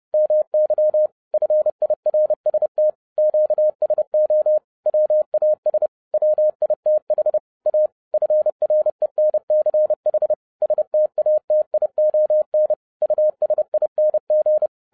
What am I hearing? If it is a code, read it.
Morse code, heard as a keyed tone, about 30 wpm, 610 Hz. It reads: MY FIRST QSO WAS WITH A FRENCH STATION USING